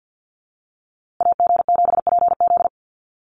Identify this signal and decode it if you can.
Morse code, heard as a keyed tone, about 50 wpm, 700 Hz. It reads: AG7PZ
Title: USA Call Signs Repeated 2x 50wpm